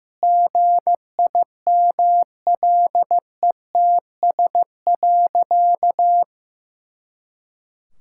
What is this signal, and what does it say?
Morse code, heard as a keyed tone, about 15 wpm, 700 Hz. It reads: GIMLETS.
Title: Pangram Challenge v2 - 15wpm